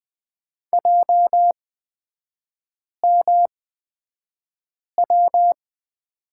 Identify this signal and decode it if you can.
Morse code, heard as a keyed tone, about 20 wpm, spaced as for 5 wpm, 700 Hz. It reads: JMW
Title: 224 Lesson 36e - BT - Three Characters 20wpm